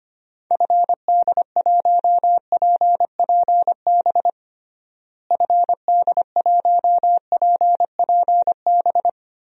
Morse code, heard as a keyed tone, about 25 wpm, 700 Hz. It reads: FD1PPB FD1PPB
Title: DXCC-100 Call Signs - 6 Characters - Repeated 3x - 25wpm